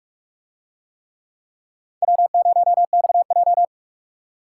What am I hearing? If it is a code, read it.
Morse code, heard as a keyed tone, about 44 wpm, 700 Hz. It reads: W0XJ